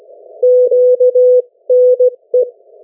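Morse code, heard as a keyed tone, about 17 wpm, 500 Hz. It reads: QNE
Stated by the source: Morse code from the national amateur radio CW net of New Zealand, which meets on 3535.0 kHz or 7030 kHz.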